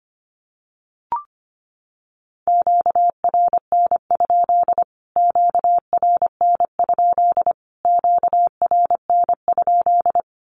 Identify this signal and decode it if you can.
Morse code, heard as a keyed tone, about 25 wpm, 700 Hz. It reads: QRN? QRN? QRN?
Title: Q-Codes Repeated 3x 25wpm